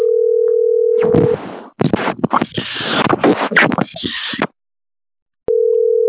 Ringback tone (North America), 440 + 480 Hz.